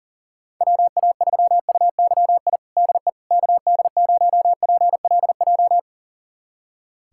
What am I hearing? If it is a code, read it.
Morse code, heard as a keyed tone, about 37 wpm, 700 Hz. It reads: WA3UYI DE KD0PLJ
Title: USA Callsign DE Callsign - 6 Characters - 40wpm